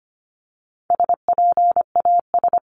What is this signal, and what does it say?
Morse code, heard as a keyed tone, about 25 wpm, 700 Hz. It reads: SPAS
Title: CWT Exchange 25wpm